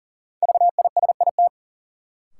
Morse code, heard as a keyed tone, about 40 wpm, 700 Hz. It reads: VISIT
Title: Sentences from Top 900 Words - New Word Drill - 40wpm